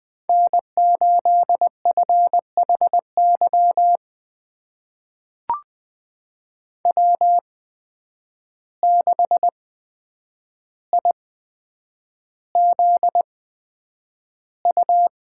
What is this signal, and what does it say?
Morse code, heard as a keyed tone, about 20 wpm, spaced as for 9 wpm, 700 Hz. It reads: N8FHY W6IZU
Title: USA Call Signs - 5 Characters - 8x Character Spacing - 20wpm